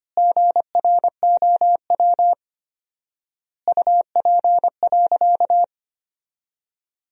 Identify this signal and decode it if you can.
Morse code, heard as a keyed tone, about 25 wpm, 700 Hz. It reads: GROW UP.
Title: 4x Word-Spacing - Sentences from Top 300 Words - 25wpm